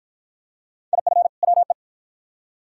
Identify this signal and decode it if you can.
Morse code, heard as a keyed tone, about 53 wpm, 700 Hz. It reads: IF WE